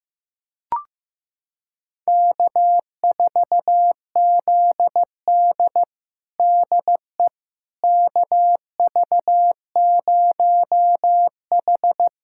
Morse code, heard as a keyed tone, about 16 wpm, 700 Hz. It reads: K4ZD DE KV0H
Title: USA Callsign DE Callsign - 4 Characters - 15wpm